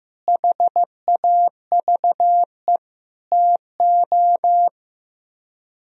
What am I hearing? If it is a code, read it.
Morse code, heard as a keyed tone, about 15 wpm, 700 Hz. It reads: HAVE TO
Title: Speed-Racing - Sets of 5 Words 15wpm